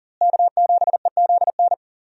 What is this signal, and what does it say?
Morse code, heard as a keyed tone, about 40 wpm, 700 Hz. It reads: K7EZN
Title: USA Call Signs - 5 Characters - Repeated 2x - 40wpm